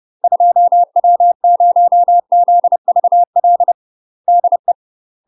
Morse code, heard as a keyed tone, about 30 wpm, 700 Hz. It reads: 2W0ZVL DE
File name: UK MASTER QSOs TRAINING / 30WPM / UK MASTER QSOs 0031